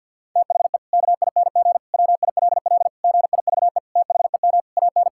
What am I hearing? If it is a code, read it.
Morse code, heard as a keyed tone, about 50 wpm, 700 Hz. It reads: THE KING WILL GIVE THEM AN
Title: Sentences from Top 1000 Words - Review Part 5 - 50wpm